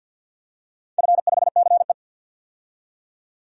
Morse code, heard as a keyed tone, about 48 wpm, 700 Hz. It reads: F5CE
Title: DXCC-100 Call Signs - 4 Characters - 50wpm